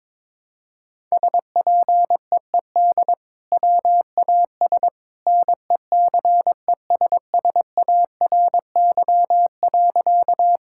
Morse code, heard as a keyed tone, about 22 wpm, 700 Hz. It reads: SPEED WAS NECESSARY.
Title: Sentences from Top 1000 Words - Review Part 5 - 22wpm